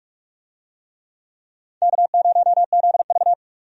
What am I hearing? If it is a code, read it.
Morse code, heard as a keyed tone, about 43 wpm, 700 Hz. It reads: K0ZV
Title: USA Call Signs - 4 Characters - 45wpm